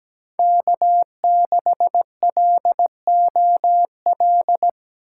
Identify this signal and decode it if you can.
Morse code, heard as a keyed tone, about 17 wpm, 700 Hz. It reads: K6LOL
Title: USA Call Signs - 4x Character Spacing - 17wpm